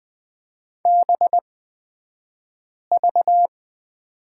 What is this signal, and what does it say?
Morse code, heard as a keyed tone, about 20 wpm, spaced as for 5 wpm, 700 Hz. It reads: BV
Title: 249 Lesson 40d - BK - Two Characters 20wpm